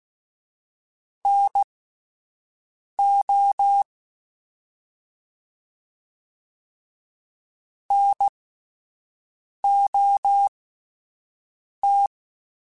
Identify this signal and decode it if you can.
Morse code, heard as a keyed tone, about 16 wpm, spaced as for 5 wpm, 800 Hz. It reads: NO NOT